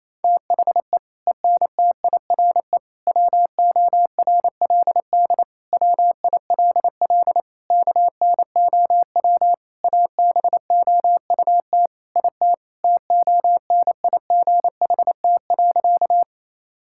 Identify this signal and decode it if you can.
Morse code, heard as a keyed tone, about 28 wpm, 700 Hz. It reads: THE ENTIRE WORLD WILL KNOW ABOUT IT TONIGHT.
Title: Sentences from Top 1000 Words - Review Part 5 - 28wpm